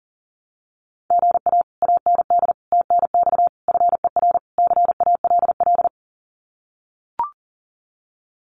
Morse code, heard as a keyed tone, about 40 wpm, 700 Hz. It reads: GA AND TNX FER CALL
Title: Common QSO Phrases Encore 40wpm